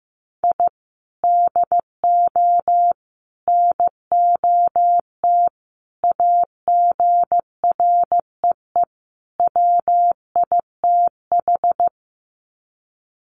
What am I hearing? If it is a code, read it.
Morse code, heard as a keyed tone, about 15 wpm, 700 Hz. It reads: I DO NOT AGREE WITH
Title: Sets of 5 Words Encore 15wpm